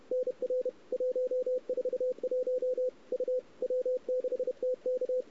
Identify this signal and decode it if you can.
Morse code, heard as a keyed tone, about 31 wpm, 510 Hz. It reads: NR 141 U W6TK
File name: CW / 2008-ss-cw-k5tr-op-n5rz -02:00